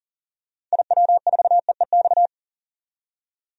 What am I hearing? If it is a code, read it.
Morse code, heard as a keyed tone, about 40 wpm, 700 Hz. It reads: IW4EEX
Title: DXCC-100 Call Signs - 6 Characters - 40wpm